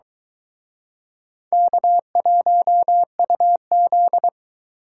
Morse code, heard as a keyed tone, about 23 wpm, 700 Hz. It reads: K1UZ